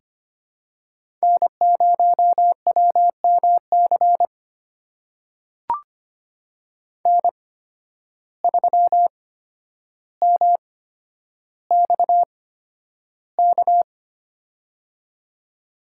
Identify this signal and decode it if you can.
Morse code, heard as a keyed tone, about 25 wpm, spaced as for 11 wpm, 700 Hz. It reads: N0WMC N3MXK
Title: USA Call Signs - 5 Characters - 8x Character Spacing - 25wpm